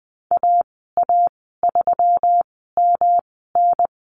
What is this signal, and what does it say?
Morse code, heard as a keyed tone, about 19 wpm, spaced as for 14 wpm, 700 Hz. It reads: AA3MN
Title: 167 Lesson 27g - Number 7 - Call Signs 20wpm